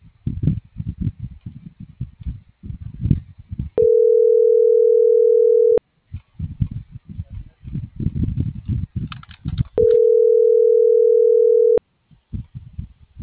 Ringback tone (North America), 440 + 480 Hz.